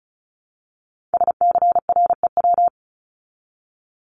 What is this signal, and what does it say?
Morse code, heard as a keyed tone, about 34 wpm, 700 Hz. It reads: SCREW
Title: Top 1501-1750 Words 35wpm